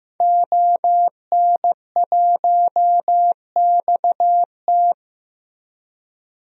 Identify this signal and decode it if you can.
Morse code, heard as a keyed tone, about 15 wpm, 700 Hz. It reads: ON1XT
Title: DXCC-100 Call Signs - 5 Characters - Repeated 2x - 15wpm